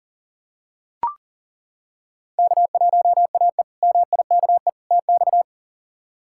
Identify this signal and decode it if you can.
Morse code, heard as a keyed tone, about 40 wpm, 700 Hz. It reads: K1AE MIKE TX
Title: K1USN SST QSO Practice - 40wpm